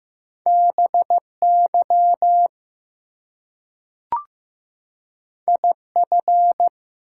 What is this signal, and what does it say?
Morse code, heard as a keyed tone, about 15 wpm, 700 Hz. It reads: BY IF
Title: Two Letter Words - 15wpm